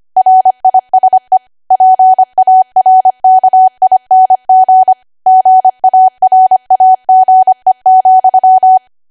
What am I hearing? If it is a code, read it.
Morse code, heard as a keyed tone, about 25 wpm, 750 Hz. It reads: RISE PARKING GARAGE,
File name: Archive / 25 WPM / 25WPM